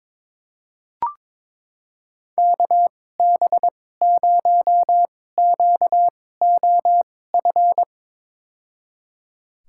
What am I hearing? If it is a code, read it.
Morse code, heard as a keyed tone, about 22 wpm, spaced as for 16 wpm, 700 Hz. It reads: KB0QOF
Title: USA Call Signs - 6 Characters - 2x Character Spacing - 22wpm